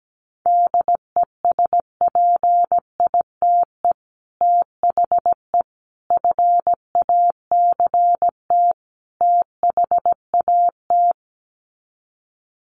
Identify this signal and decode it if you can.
Morse code, heard as a keyed tone, about 17 wpm, 700 Hz. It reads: DESPITE THE FACT THAT
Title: Sets of 4 Words 17wpm